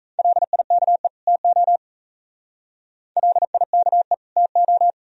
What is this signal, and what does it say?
Morse code, heard as a keyed tone, about 35 wpm, 700 Hz. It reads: LIKE TO LIKE TO